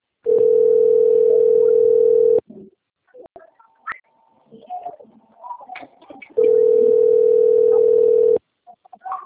Ringback tone (North America), 440 + 480 Hz.
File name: rg-1600-unknown- - - .53681